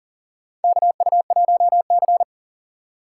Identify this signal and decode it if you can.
Morse code, heard as a keyed tone, about 37 wpm, 700 Hz. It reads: KU1C
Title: USA Call Signs - 4 Characters - 4x Character Spacing - 40wpm